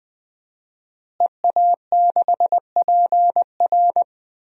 Morse code, heard as a keyed tone, about 20 wpm, 700 Hz. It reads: EA6PR
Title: DXCC-100 Call Signs - 5 Characters - Repeated 3x - 20wpm